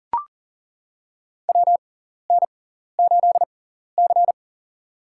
Morse code, heard as a keyed tone, about 40 wpm, spaced as for 14 wpm, 700 Hz. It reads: WN8C
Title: USA Call Signs - 4 Characters - 6x Character Spacing - 40wpm